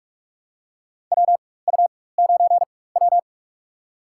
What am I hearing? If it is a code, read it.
Morse code, heard as a keyed tone, about 43 wpm, spaced as for 21 wpm, 700 Hz. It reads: WU9W
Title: USA Call Signs - 4 Characters - 4x Character Spacing - 45wpm